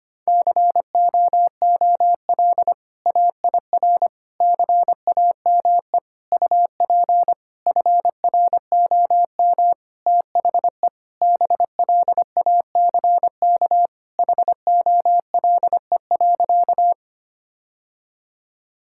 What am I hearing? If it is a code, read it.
Morse code, heard as a keyed tone, about 25 wpm, 700 Hz. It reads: COOL AIR CAME UP FROM THE BLACK HOLE.
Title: Sentences from Top 1600 Words 25wpm